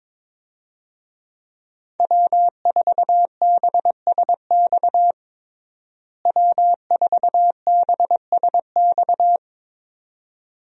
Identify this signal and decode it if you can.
Morse code, heard as a keyed tone, about 22 wpm, 700 Hz. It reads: W4BSX W4BSX
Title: USA Call Signs - 5 Characters - Repeated 2x - 22wpm